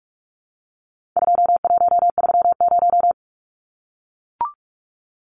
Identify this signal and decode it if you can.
Morse code, heard as a keyed tone, about 44 wpm, 700 Hz. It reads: 2130
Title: Numbers - 4-Digit 45wpm